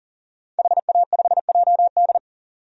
Morse code, heard as a keyed tone, about 40 wpm, 700 Hz. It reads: HA5JD
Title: DXCC-100 Call Signs - 5 Characters - 40wpm